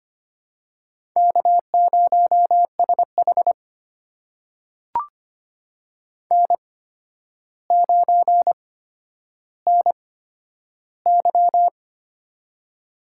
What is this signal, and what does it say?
Morse code, heard as a keyed tone, about 25 wpm, spaced as for 11 wpm, 700 Hz. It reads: K0SH N9NY
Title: USA Call Signs - 4 Characters - 8x Character Spacing - 25wpm